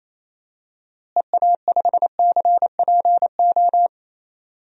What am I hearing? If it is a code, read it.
Morse code, heard as a keyed tone, about 28 wpm, 700 Hz. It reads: EA5CPO
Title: DXCC-100 Call Signs - 6 Characters - Repeated 2x - 28wpm